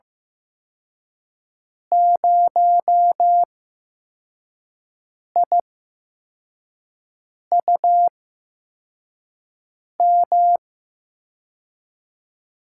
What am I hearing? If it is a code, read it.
Morse code, heard as a keyed tone, about 15 wpm, spaced as for 4 wpm, 700 Hz. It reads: 0IUM
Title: USA Call Signs - 8x Character Spacing - 15wpm